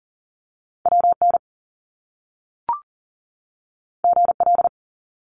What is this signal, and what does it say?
Morse code, heard as a keyed tone, about 42 wpm, 700 Hz. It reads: WN GL
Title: Top Two Letter Combinations in Words 40wpm